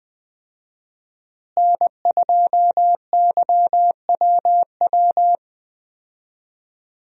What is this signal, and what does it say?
Morse code, heard as a keyed tone, about 21 wpm, 700 Hz. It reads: N2YWW
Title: USA Call Signs - 5 Characters - Repeated 2x - 20wpm